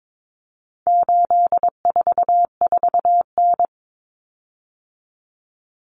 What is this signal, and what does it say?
Morse code, heard as a keyed tone, about 22 wpm, 700 Hz. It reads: 844N
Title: Common Cut Numbers - 4-Digit 22wpm